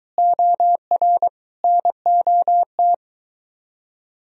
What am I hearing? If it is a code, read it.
Morse code, heard as a keyed tone, about 23 wpm, 700 Hz. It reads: OR NOT